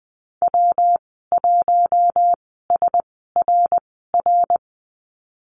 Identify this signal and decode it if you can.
Morse code, heard as a keyed tone, about 20 wpm, spaced as for 15 wpm, 700 Hz. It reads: W1SRR
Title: 147 Lesson 24g - Letter y - Call Signs 20wpm